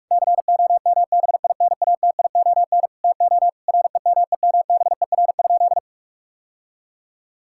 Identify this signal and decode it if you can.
Morse code, heard as a keyed tone, about 46 wpm, 700 Hz. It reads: COMBINATION TO REMEMBER?
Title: Sentences from Top 1000 Words - Review Part 6 - 45wpm